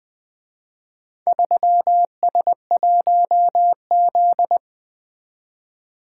Morse code, heard as a keyed tone, about 23 wpm, 700 Hz. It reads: 3S1Z
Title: Speed-Racing - Four Letter-Number Combinations - 20wpm